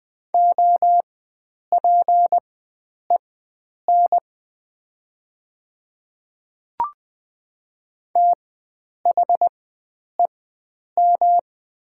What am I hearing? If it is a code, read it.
Morse code, heard as a keyed tone, about 20 wpm, spaced as for 9 wpm, 700 Hz. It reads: OPEN THEM
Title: 4x Character Spacing - Top 300 Words - 20wpm